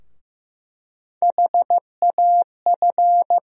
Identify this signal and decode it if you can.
Morse code, heard as a keyed tone, about 15 wpm, 700 Hz. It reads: HAF
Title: USA Call Sign Postfixes - 3 Letters 15wpm